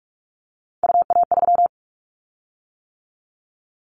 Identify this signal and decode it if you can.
Morse code, heard as a keyed tone, about 46 wpm, 700 Hz. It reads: UA3
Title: Callsign Prefixes for Top 50 DXCC - 45wpm